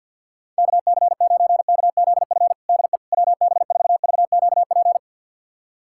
Morse code, heard as a keyed tone, about 46 wpm, 700 Hz. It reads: KC9KZF DE WB4VQP
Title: USA Callsign DE Callsign - 6 Characters - 50wpm